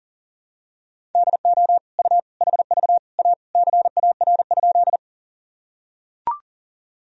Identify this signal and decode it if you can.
Morse code, heard as a keyed tone, about 40 wpm, 700 Hz. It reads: DO U HV A CAR?
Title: Sentences from Top 300 Words and Abbreviations 40wpm